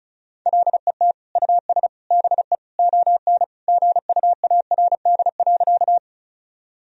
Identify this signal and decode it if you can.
Morse code, heard as a keyed tone, about 34 wpm, 700 Hz. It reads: LET US BE ON GUARD.
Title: Sentences from Top 1000 Words - Review Part 4 - 35wpm